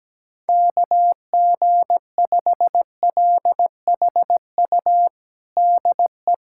Morse code, heard as a keyed tone, about 17 wpm, 700 Hz. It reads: KG5LHU DE
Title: USA Callsign DE Callsign - 6 Characters - 17wpm